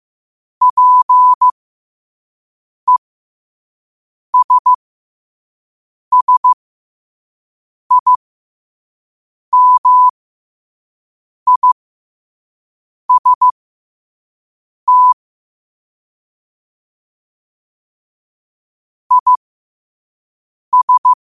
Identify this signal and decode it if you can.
Morse code, heard as a keyed tone, about 15 wpm, spaced as for 5 wpm, 1000 Hz. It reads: PESSIMIST IS
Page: New quotes every day in morse code at 5 Words per minute.